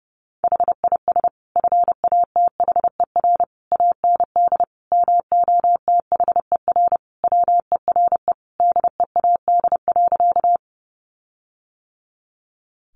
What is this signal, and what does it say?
Morse code, heard as a keyed tone, about 28 wpm, 700 Hz. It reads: HIS FATHER AND MOTHER WERE DEAD.
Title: Sentences from Top 1500 Words 30wpm